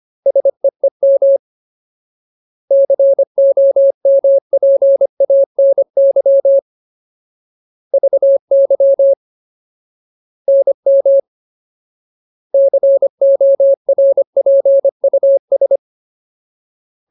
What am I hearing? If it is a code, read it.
Morse code, heard as a keyed tone, about 25 wpm, 550 Hz. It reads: SEEM COMPANY VY NM CORPUS
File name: Ultimate Word List / 25WPM Ultimate